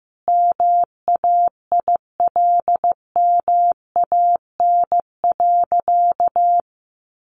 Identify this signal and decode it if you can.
Morse code, heard as a keyed tone, about 14 wpm, 700 Hz. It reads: MAILMAN.
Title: Additional Easy Phrases 15wpm